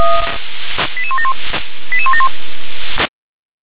Five-tone selective call (ZVEI), heard as quick runs of five tones, tones of about 70 ms. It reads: 88171 88171